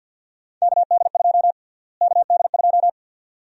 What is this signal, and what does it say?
Morse code, heard as a keyed tone, about 52 wpm, 700 Hz. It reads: KD2 KD2
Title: USA Call Sign Prefixes - Repeated 2x - 50wpm